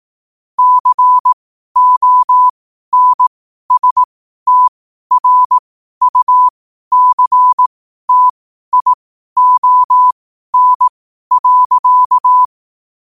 Morse code, heard as a keyed tone, about 18 wpm, spaced as for 12 wpm, 1000 Hz. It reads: CONSTRUCTION.
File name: qotdfiles / 12wpm